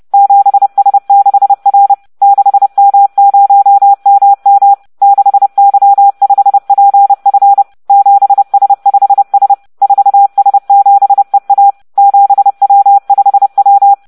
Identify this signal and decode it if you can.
Morse code, heard as a keyed tone, about 30 wpm, 800 Hz. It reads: 7S6R 6M0MM 6Y5PF 7S5S 4S7EA 7W5W